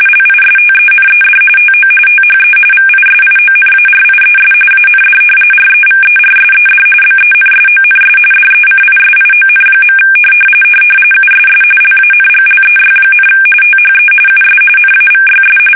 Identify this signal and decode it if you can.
RTTY, about 45.45 baud. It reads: LISTENED TO THE VISIONARIES AND REVERSED BENSONS DECISION. USS LANGLEY, THE US NAVYS
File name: fsk850